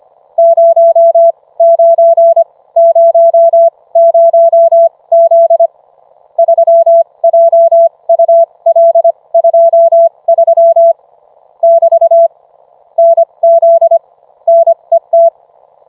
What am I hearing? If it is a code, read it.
Morse code, heard as a keyed tone, about 25 wpm, spaced as for 20 wpm, 670 Hz. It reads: 0900Z 3JUL23 <BT> NZ NET